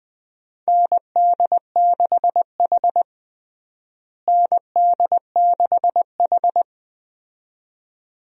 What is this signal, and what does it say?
Morse code, heard as a keyed tone, about 20 wpm, 700 Hz. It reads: ND6H ND6H
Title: USA Call Signs - 4 Characters - Repeated 2x - 20wpm